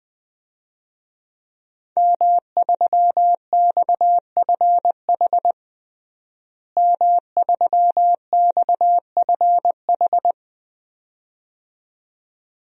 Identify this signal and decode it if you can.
Morse code, heard as a keyed tone, about 20 wpm, 700 Hz. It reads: M3XFH M3XFH
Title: DXCC-100 Call Signs - 5 Characters - Repeated 2x - 20wpm